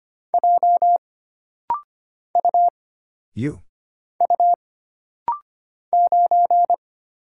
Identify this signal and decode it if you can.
Morse code, heard as a keyed tone, about 25 wpm, 700 Hz. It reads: J U U 9
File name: Single Character (Letter:Number:Punctuation:Prosign) - Rapid-Fire - 25wpm